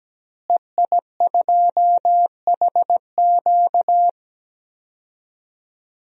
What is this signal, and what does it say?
Morse code, heard as a keyed tone, about 17 wpm, 700 Hz. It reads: EI2HQ
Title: DXCC-100 Call Signs - 5 Characters - Repeated 3x - 17wpm